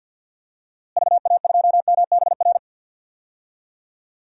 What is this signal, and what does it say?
Morse code, heard as a keyed tone, about 52 wpm, 700 Hz. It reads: VA2MBR
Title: DXCC-100 Call Signs - 6 Characters - Repeated 3x - 50wpm